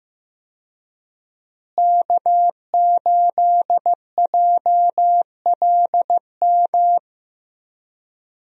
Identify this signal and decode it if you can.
Morse code, heard as a keyed tone, about 15 wpm, 700 Hz. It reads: K8JLM
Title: USA Call Signs - 5 Characters - Repeated 2x - 15wpm